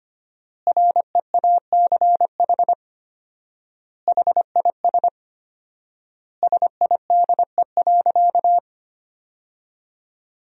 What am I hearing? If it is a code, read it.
Morse code, heard as a keyed tone, about 25 wpm, 700 Hz. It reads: REACH HIS SIDE.
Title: 4x Word-Spacing - Sentences from Top 400 Words - 25wpm